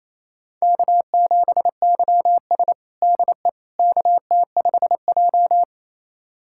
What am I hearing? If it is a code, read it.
Morse code, heard as a keyed tone, about 28 wpm, 700 Hz. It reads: K7YS DE KT5J
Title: USA Callsign DE Callsign - 4 Characters - 28wpm